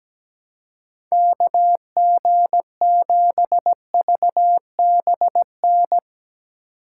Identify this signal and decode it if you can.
Morse code, heard as a keyed tone, about 17 wpm, 700 Hz. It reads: KG7VBN
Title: Speed Racing - USA 2x3 Call Signs 17wpm